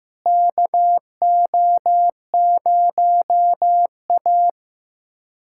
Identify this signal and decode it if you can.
Morse code, heard as a keyed tone, about 15 wpm, 700 Hz. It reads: KO0A
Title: USA Callsign DE Callsign - 4 Characters - 15wpm